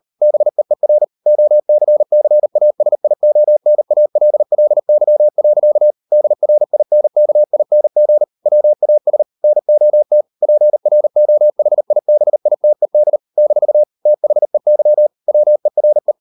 Morse code, heard as a keyed tone, about 38 wpm, 600 Hz. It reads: BEER OCCASIONALLY. DRINKING WAS NOT PROHIBITED - THEY WERE